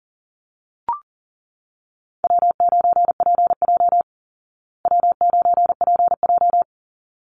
Morse code, heard as a keyed tone, about 40 wpm, 700 Hz. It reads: W9PJ W9PJ
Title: Call Signs Repeated 3x - 40wpm